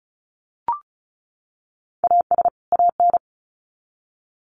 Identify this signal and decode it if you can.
Morse code, heard as a keyed tone, about 34 wpm, 700 Hz. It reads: AS AN
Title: Sets of 2 Words 35wpm